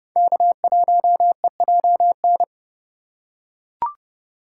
Morse code, heard as a keyed tone, about 32 wpm, 700 Hz. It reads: K1EJN
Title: USA Call Signs - 3x Character Spacing - 30wpm